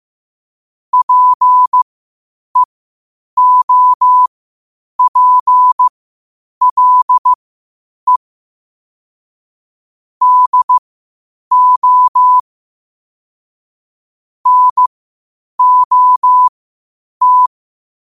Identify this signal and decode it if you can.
Morse code, heard as a keyed tone, about 15 wpm, spaced as for 8 wpm, 1000 Hz. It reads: PEOPLE DO NOT
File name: qotdfiles / 8wpm